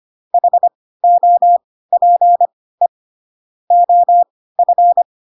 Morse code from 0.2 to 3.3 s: HOPE